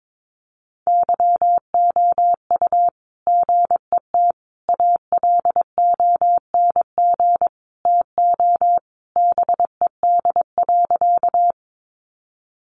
Morse code, heard as a keyed tone, about 22 wpm, 700 Hz. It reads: YOU GET ALONG TO BED.
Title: Sentences from Top 700 Words 22wpm